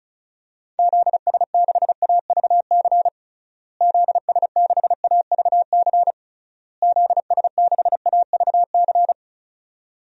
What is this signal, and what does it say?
Morse code, heard as a keyed tone, about 36 wpm, 700 Hz. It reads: ZS6AVC ZS6AVC ZS6AVC
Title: DXCC-100 Call Signs - 6 Characters - Repeated 3x - 35wpm